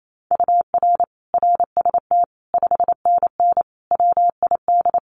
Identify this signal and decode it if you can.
Morse code, heard as a keyed tone, about 28 wpm, 700 Hz. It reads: UR RST 5NN WID
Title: Common QSO Phrases 28wpm